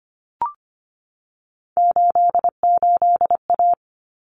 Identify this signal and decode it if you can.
Morse code, heard as a keyed tone, about 25 wpm, 700 Hz. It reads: 88A
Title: Common Cut Numbers - 3-Digit 25wpm